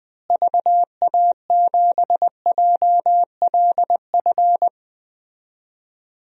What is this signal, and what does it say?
Morse code, heard as a keyed tone, about 20 wpm, 700 Hz. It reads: VA7JLF
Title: DXCC-100 Call Signs - 6 Characters - Repeated 2x - 20wpm